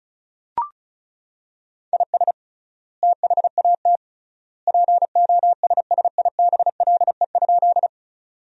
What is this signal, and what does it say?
Morse code, heard as a keyed tone, about 34 wpm, 700 Hz. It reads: IS THAT POSSIBLE?
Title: 3x Word-Spacing - Sentences from Top 500 Words - 35wpm